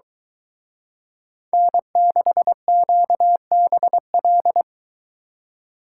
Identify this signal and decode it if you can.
Morse code, heard as a keyed tone, about 22 wpm, 700 Hz. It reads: N6QBL